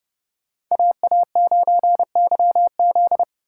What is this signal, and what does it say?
Morse code, heard as a keyed tone, about 31 wpm, 700 Hz. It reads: AA9YZ